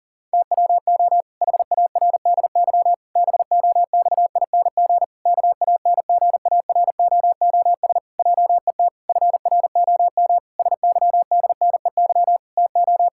Morse code, heard as a keyed tone, about 40 wpm, 700 Hz. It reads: TWO HARDY BOXING KANGAROOS JET FROM SYDNEY TO
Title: Pangram Challenge v2 - 40wpm